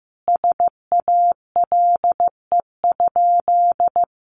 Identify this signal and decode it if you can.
Morse code, heard as a keyed tone, about 15 wpm, 700 Hz. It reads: SALE?
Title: Sentences from Top 1400 Words 15wpm